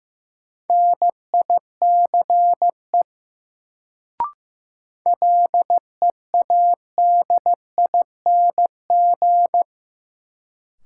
Morse code, heard as a keyed tone, about 15 wpm, 700 Hz. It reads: NICE LEADING
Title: Sentences from Top 1100 Words - New Word Drill Alt - 15wpm